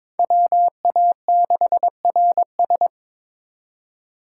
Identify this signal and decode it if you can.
Morse code, heard as a keyed tone, about 26 wpm, 700 Hz. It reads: WA6RS
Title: Speed Racing - USA 2x2 Call Signs 22wpm